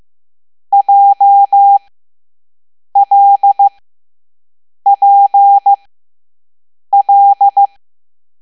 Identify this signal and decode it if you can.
Morse code, heard as a keyed tone, about 15 wpm, spaced as for 6 wpm, 800 Hz. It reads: JLPL